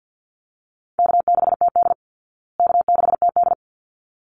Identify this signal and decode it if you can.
Morse code, heard as a keyed tone, about 48 wpm, 700 Hz. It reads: K6TD K6TD
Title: Easy Call Signs Repeated 2x - 50wpm